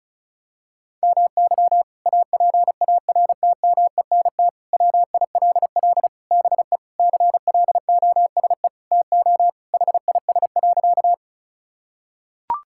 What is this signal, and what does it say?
Morse code, heard as a keyed tone, about 36 wpm, 700 Hz. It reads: MY APARTMENT WILL BE CLOSE TO HIS.
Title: Sentences from Top 1000 Words - Review Part 6 - 35wpm